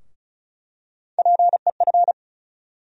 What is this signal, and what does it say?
Morse code, heard as a keyed tone, about 34 wpm, 700 Hz. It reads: PEF
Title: USA Call Sign Postfixes - 3 Letters 35wpm